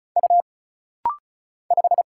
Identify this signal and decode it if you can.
Morse code, heard as a keyed tone, about 34 wpm, 700 Hz. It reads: U 5
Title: Single Letter-Number - Rapid-Fire 35wpm